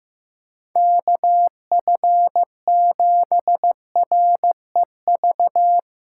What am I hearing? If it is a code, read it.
Morse code, heard as a keyed tone, about 15 wpm, 700 Hz. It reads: KF7REV
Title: USA Callsign DE Callsign - 6 Characters - 15wpm